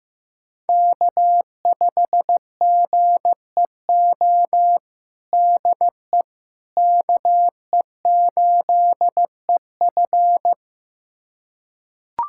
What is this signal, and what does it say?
Morse code, heard as a keyed tone, about 15 wpm, 700 Hz. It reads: K5GEO DE KE8EF
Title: USA Callsign DE Callsign - 5 Characters - 15wpm